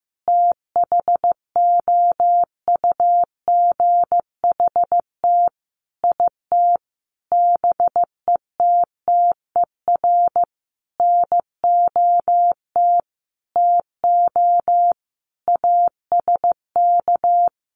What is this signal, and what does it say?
Morse code, heard as a keyed tone, about 15 wpm, 700 Hz. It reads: THOUGHT IT BETTER NOT TO ASK
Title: Sentences from Top 600 Words 15wpm